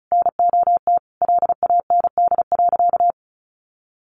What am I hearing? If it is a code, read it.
Morse code, heard as a keyed tone, about 34 wpm, 700 Hz. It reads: NOT LAND.
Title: Sentences from Top 1500 Words 35wpm